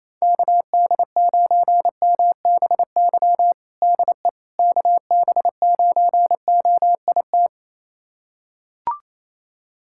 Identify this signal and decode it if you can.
Morse code, heard as a keyed tone, about 28 wpm, 700 Hz. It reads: KD9MBY DE KB9OIT
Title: USA Callsign DE Callsign 28wpm